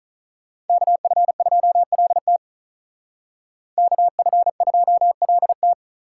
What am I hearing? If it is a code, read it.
Morse code, heard as a keyed tone, about 43 wpm, 700 Hz. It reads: KF2LT KF2LT
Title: Speed Racing - USA 2x2 Call Signs 35wpm